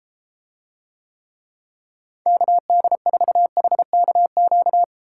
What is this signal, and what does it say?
Morse code, heard as a keyed tone, about 32 wpm, 700 Hz. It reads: KD4HKQ